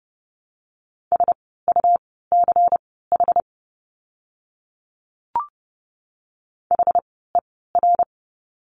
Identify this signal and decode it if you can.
Morse code, heard as a keyed tone, about 30 wpm, spaced as for 17 wpm, 700 Hz. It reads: SUCH HER
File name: course / 30:10wpm / 237 Lesson 38f - Letter q - Words 30wpm